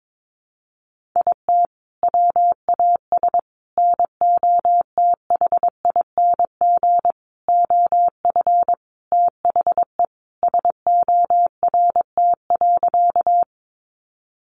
Morse code, heard as a keyed tone, about 22 wpm, 700 Hz. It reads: IT WAS NOTHING OF THE SORT.
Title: Sentences from Top 1300 Words 22wpm